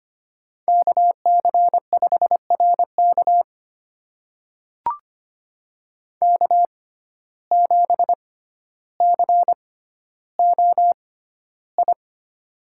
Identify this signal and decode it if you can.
Morse code, heard as a keyed tone, about 25 wpm, spaced as for 13 wpm, 700 Hz. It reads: KC5RK K7COI
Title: USA Call Signs - 5 Characters - 6x Character Spacing - 25wpm